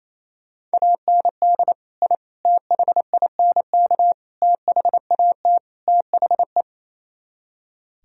Morse code, heard as a keyed tone, about 37 wpm, 700 Hz. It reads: AND I THINK THAT THE